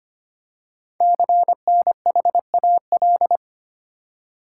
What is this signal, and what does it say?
Morse code, heard as a keyed tone, about 25 wpm, 700 Hz. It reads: CNHAL
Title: Code Groups - 5 Letters - 25wpm